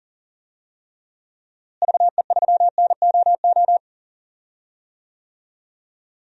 Morse code, heard as a keyed tone, about 37 wpm, 700 Hz. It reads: VE3NOO
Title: DXCC-100 Call Signs - 6 Characters - Repeated 2x - 40wpm